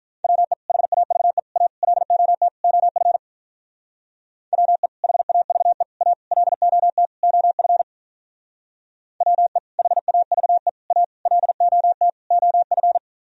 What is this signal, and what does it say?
Morse code, heard as a keyed tone, about 45 wpm, 700 Hz. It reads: WE HAVE A LOT OF WE HAVE A LOT OF WE HAVE A LOT OF